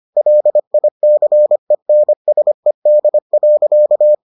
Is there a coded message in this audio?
Yes: Morse code from 0.2 to 4.1 s, heard as a keyed tone, about 25 wpm, 600 Hz. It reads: LICENSED.